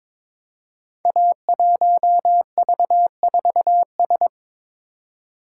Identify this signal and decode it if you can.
Morse code, heard as a keyed tone, about 22 wpm, 700 Hz. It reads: A1V4S
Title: Code Groups - 5 Letter:Numbers - 22wpm